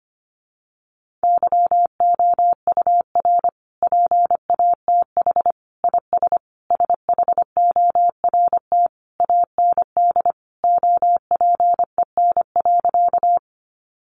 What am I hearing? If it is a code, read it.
Morse code, heard as a keyed tone, about 25 wpm, 700 Hz. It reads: YOUR PATH IS SHORT AND OPEN.
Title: Sentences from Top 1400 Words 25wpm